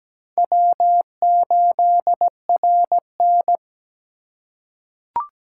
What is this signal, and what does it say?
Morse code, heard as a keyed tone, about 17 wpm, 700 Hz. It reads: W8RN
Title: USA Call Signs - 4 Characters - 4x Character Spacing - 17wpm